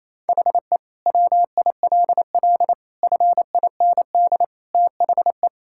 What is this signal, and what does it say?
Morse code, heard as a keyed tone, about 28 wpm, 700 Hz. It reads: HE WILL FIND THE
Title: Sentences from Top 1000 Words - Review Part 6 - 28wpm